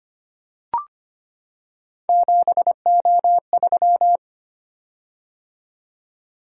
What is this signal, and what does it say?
Morse code, heard as a keyed tone, about 25 wpm, 700 Hz. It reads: 7O3
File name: Three Letter-Number Combinations - 25wpm